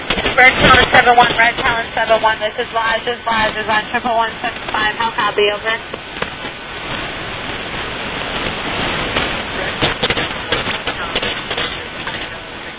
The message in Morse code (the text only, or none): none